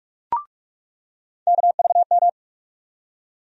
Morse code, heard as a keyed tone, about 46 wpm, 700 Hz. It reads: KVM
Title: USA Call Sign Postfixes - 3 Letters 45wpm